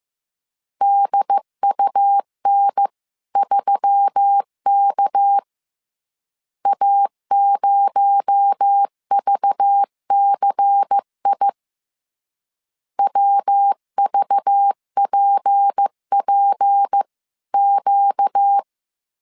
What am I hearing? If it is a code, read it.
Morse code, heard as a keyed tone, about 15 wpm, 800 Hz. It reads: DUN3K A0VCI WVPPQ